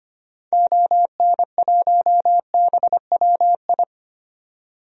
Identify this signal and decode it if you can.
Morse code, heard as a keyed tone, about 25 wpm, 700 Hz. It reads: ON1BWI
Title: DXCC-100 Call Signs - 6 Characters - 25wpm